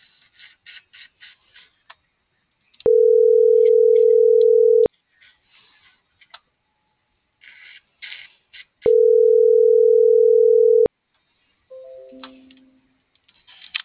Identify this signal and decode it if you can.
Ringback tone (North America), 440 + 480 Hz.